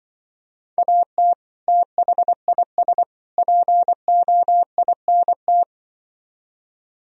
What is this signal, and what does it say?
Morse code, heard as a keyed tone, about 24 wpm, 700 Hz. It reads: AT THIS POINT
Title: Speed-Racing - Sets of 3 Words 20wpm